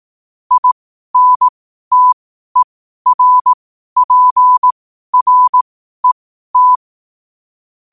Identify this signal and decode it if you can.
Morse code, heard as a keyed tone, about 18 wpm, spaced as for 12 wpm, 1000 Hz. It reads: INTERPRET
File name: qotdfiles / 12wpm